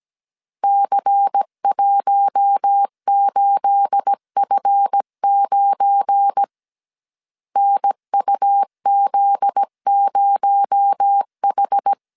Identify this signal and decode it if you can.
Morse code, heard as a keyed tone, about 17 wpm, 800 Hz. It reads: C18F9 NUZ0H
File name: CW / DET / 1 9 1500C S A 16wpm 2828